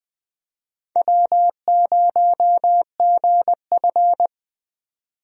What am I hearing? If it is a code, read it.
Morse code, heard as a keyed tone, about 20 wpm, 700 Hz. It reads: W0GF
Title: USA Call Signs - 4 Characters - 20wpm